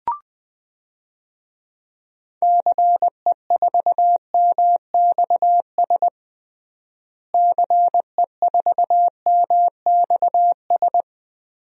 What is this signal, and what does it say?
Morse code, heard as a keyed tone, about 20 wpm, 700 Hz. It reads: CE4MXS CE4MXS
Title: DXCC-100 Call Signs - 6 Characters - Repeated 3x - 20wpm